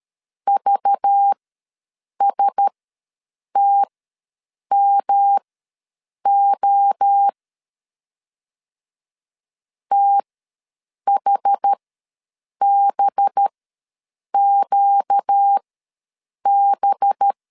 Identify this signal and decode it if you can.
Morse code, heard as a keyed tone, about 13 wpm, spaced as for 7 wpm, 800 Hz. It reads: VSTMO THBQB
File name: LEZIONI / 5 LBVC / 5 LBV ws 12wpm 3015